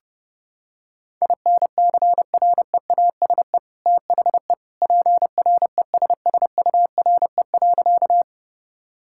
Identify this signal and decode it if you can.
Morse code, heard as a keyed tone, about 30 wpm, 700 Hz. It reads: INCREASE THE PRESSURE.
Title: Sentences from Top 1000 Words - Review Part 6 - 30wpm